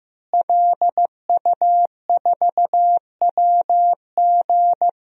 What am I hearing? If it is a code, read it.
Morse code, heard as a keyed tone, about 15 wpm, 700 Hz. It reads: LU4WG
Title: DXCC-100 Call Signs - 5 Characters - Repeated 3x - 15wpm